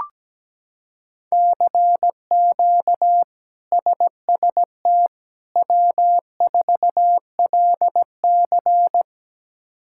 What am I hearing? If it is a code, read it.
Morse code, heard as a keyed tone, about 17 wpm, 700 Hz. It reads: CQ SST W4LC
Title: K1USN SST QSO Practice - 17wpm